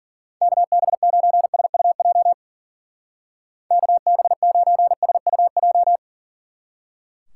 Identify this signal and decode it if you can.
Morse code, heard as a keyed tone, about 49 wpm, 700 Hz. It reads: KB9SUJ KB9SUJ
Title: Speed Racing - USA 2x3 Call Signs 40wpm